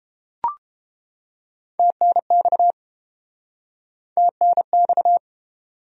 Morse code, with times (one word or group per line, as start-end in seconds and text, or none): 1.8-2.7 TNX
4.2-5.2 TNX